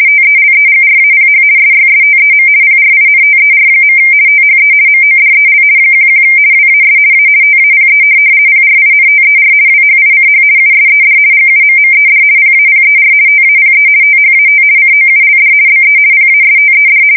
RTTY, about 50 baud. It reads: TELEGRAPH SYSTEM PRINTER. THE MODEL 11 WAS A TAPE PRINTER WHICH USED GUMMED PAPER TAPE THAT COULD BE PASTED
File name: fsk170-50B / teletype-corp